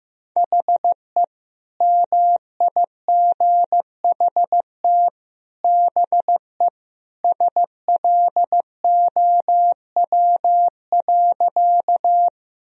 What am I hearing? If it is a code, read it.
Morse code, heard as a keyed tone, about 15 wpm, 700 Hz. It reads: HE MIGHT BE SLOW.
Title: Sentences from Top 1000 Words - Review Part 2 - 15wpm